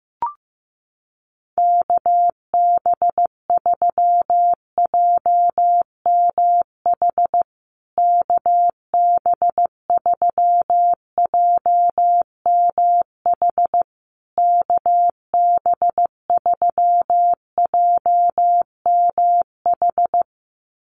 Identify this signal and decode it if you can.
Morse code, heard as a keyed tone, about 15 wpm, 700 Hz. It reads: KB3JMH KB3JMH KB3JMH
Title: USA Call Signs Repeated 3x 15wpm